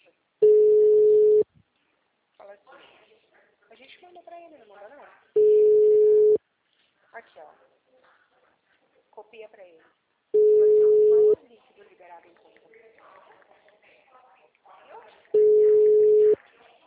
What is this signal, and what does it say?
Ringback tone (Europe, 425 Hz), with speech.
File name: out- -2013- - - .82467